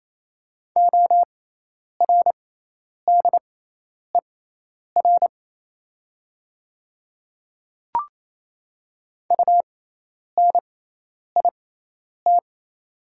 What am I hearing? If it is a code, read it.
Morse code, heard as a keyed tone, about 28 wpm, spaced as for 10 wpm, 700 Hz. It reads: ORDER UNIT
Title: 6x Character Spacing - Top 500 Words - 28wpm